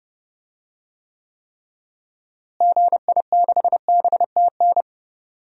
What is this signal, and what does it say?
Morse code, heard as a keyed tone, about 30 wpm, 700 Hz. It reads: GI6BTN
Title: DXCC-100 Call Signs - 6 Characters - Repeated 3x - 30wpm